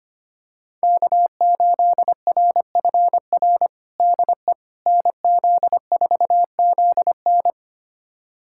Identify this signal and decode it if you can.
Morse code, heard as a keyed tone, about 25 wpm, 700 Hz. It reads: K8RFR DE NZ4ZN
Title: USA Callsign DE Callsign - 5 Characters - 25wpm